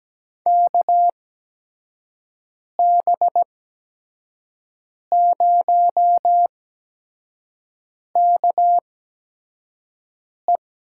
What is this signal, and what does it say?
Morse code, heard as a keyed tone, about 17 wpm, spaced as for 5 wpm, 700 Hz. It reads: KB0KE
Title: USA Call Signs - 5 Characters - 8x Character Spacing - 17wpm